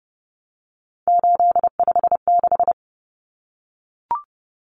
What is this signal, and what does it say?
Morse code, heard as a keyed tone, about 30 wpm, 700 Hz. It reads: 856
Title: Numbers - 3-Digit 30wpm